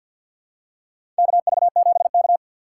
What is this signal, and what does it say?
Morse code, heard as a keyed tone, about 52 wpm, 700 Hz. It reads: KV7K
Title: USA Call Signs - 2x Character Spacing - 50wpm